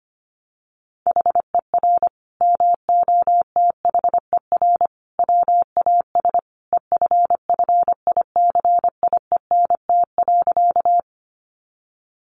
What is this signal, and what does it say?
Morse code, heard as a keyed tone, about 25 wpm, 700 Hz. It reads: HER MOTHER WAS EFFICIENT.
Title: Sentences from Top 1300 Words 25wpm